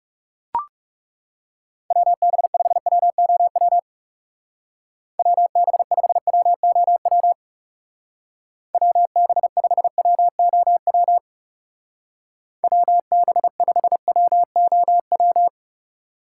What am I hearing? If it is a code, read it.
Morse code, heard as a keyed tone, about 36 wpm, 700 Hz. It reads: WB5WOW WB5WOW WB5WOW WB5WOW
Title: Speed Racing - USA 2x3 Call Signs 30wpm